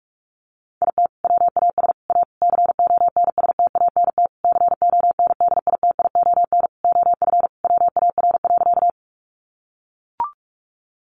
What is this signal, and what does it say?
Morse code, heard as a keyed tone, about 44 wpm, 700 Hz. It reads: IT WAS A CONSTANT CONDITION OF WAR.
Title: Sentences from Top 1500 Words 45wpm